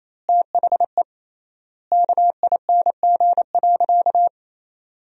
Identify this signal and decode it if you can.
Morse code, heard as a keyed tone, about 27 wpm, 700 Hz. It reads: THE KING.
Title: 3x Word-Spacing - Sentences from Top 400 Words - 28wpm